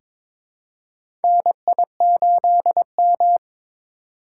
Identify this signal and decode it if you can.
Morse code, heard as a keyed tone, about 22 wpm, 700 Hz. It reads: NI8M
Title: USA Call Signs - 4 Characters - 3x Character Spacing - 22wpm